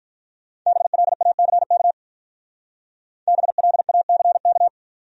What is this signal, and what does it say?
Morse code, heard as a keyed tone, about 45 wpm, 700 Hz. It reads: BLACK BLACK